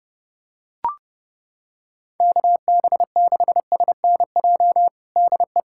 Morse code, heard as a keyed tone, about 30 wpm, 700 Hz. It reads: KB6SNJ DE
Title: USA Callsign DE Callsign - 6 Characters - 30wpm